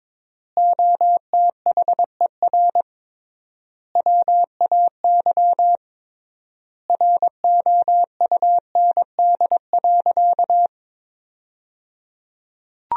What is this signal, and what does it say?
Morse code, heard as a keyed tone, about 22 wpm, 700 Hz. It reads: OTHER WAY ROUND.
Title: 3x Word-Spacing - Sentences from Top 200 Words - 22wpm